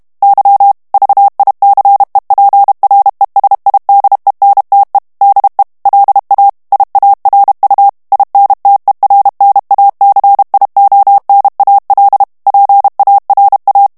Morse code, heard as a keyed tone, about 32 wpm, 800 Hz. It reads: Y VICEPRESIDENTE DE LA IARU INTERNACIONAL PARA
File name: morse mp3 / ESPAGNOL TEXTE EN CLAIR 1 VITESSE 30 MOTS MINUTES